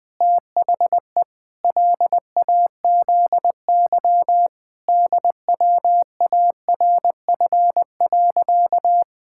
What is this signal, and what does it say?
Morse code, heard as a keyed tone, about 20 wpm, 700 Hz. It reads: THE LAZY DWARF.
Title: Pangram Challenge v2 - 20wpm